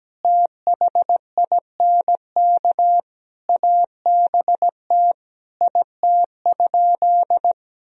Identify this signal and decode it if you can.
Morse code, heard as a keyed tone, about 17 wpm, 700 Hz. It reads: THINK ABT IT?
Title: Sentences from Top 200 Words and Abbreviations 17wpm